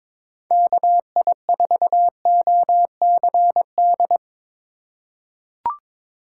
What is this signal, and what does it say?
Morse code, heard as a keyed tone, about 22 wpm, 700 Hz. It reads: KI4OCD
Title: USA Call Signs - 6 Characters - 4x Character Spacing - 22wpm